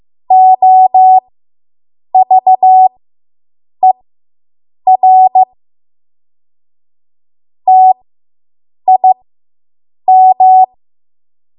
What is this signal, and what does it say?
Morse code, heard as a keyed tone, about 15 wpm, spaced as for 7 wpm, 750 Hz. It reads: OVER TIM